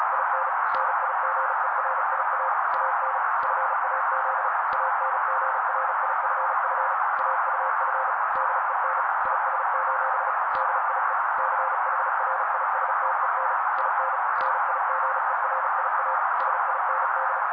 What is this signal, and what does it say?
Morse code, heard as a keyed tone, about 35 wpm, 530 Hz. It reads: IN NIGERIA AN FAZ ANGRIFF AUF DEN IS: DIE CHRISTEN IN NIGERIA SIND
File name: daily / 35wpm noise